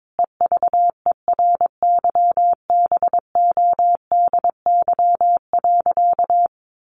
Morse code, heard as a keyed tone, about 22 wpm, 700 Hz. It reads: EVERYBODY.
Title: Sentences from Top 1400 Words 22wpm